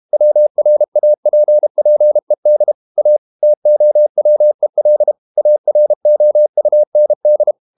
Morse code, heard as a keyed tone, about 32 wpm, 600 Hz. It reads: WRAPPED A TOWEL AROUND
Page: Never 31wpm | CW med Gnister